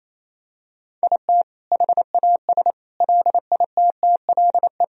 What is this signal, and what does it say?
Morse code, heard as a keyed tone, about 28 wpm, 700 Hz. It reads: IT HAS LITTLE
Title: Sentences from Top 1000 Words - Review Part 4 - 28wpm